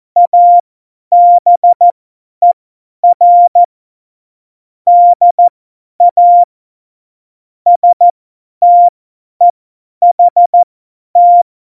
Morse code, heard as a keyed tone, about 14 wpm, spaced as for 10 wpm, 700 Hz. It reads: ABER DA STEHT
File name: Morsekurs / Witze / Witz 09. Oktober 2018, 50 F70